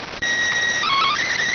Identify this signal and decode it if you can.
SSTV (slow-scan television), PD 290.